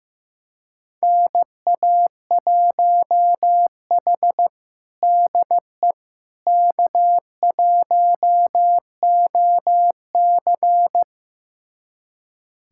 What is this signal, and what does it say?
Morse code, heard as a keyed tone, about 15 wpm, 700 Hz. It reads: NA1H DE K1OC
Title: USA Callsign DE Callsign - 4 Characters - 15wpm